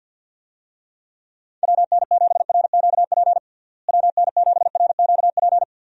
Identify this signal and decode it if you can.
Morse code, heard as a keyed tone, about 52 wpm, 700 Hz. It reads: WN7RQP WN7RQP
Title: USA Call Signs - 6 Characters - Repeated 2x - 50wpm